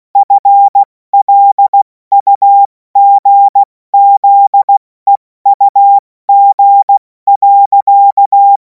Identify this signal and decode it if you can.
Morse code, heard as a keyed tone, about 16 wpm, spaced as for 14 wpm, 800 Hz. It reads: FLUGZEUG.
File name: Morsekurs / Witze / Witz 31. März 2026 70 F80